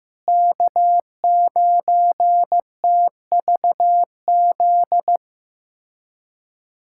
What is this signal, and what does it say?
Morse code, heard as a keyed tone, about 15 wpm, 700 Hz. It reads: K9TVZ
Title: USA Callsign DE Callsign - 5 Characters - 15wpm